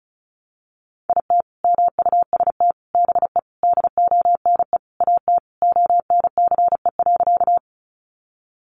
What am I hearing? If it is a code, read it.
Morse code, heard as a keyed tone, about 34 wpm, 700 Hz. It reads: IT MUST BE DONE AT ONCE.
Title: Sentences from Top 500 Words 35wpm